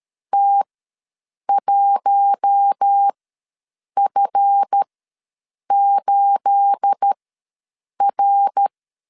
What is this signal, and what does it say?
Morse code, heard as a keyed tone, about 13 wpm, spaced as for 7 wpm, 800 Hz. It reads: T1F8R
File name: LEZIONI / 9 28 ws 12wpm 3354